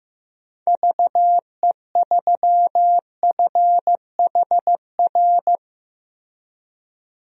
Morse code, heard as a keyed tone, about 15 wpm, 700 Hz. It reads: VE3FHR
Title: DXCC-100 Call Signs - 6 Characters - Repeated 2x - 15wpm